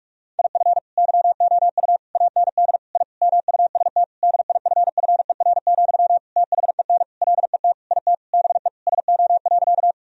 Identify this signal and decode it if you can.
Morse code, heard as a keyed tone, about 46 wpm, 700 Hz. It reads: IF YOU AND I MUST DIFFER, THEN LET IT BE SO.
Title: Sentences from Top 1000 Words - Review Part 1 - 45wpm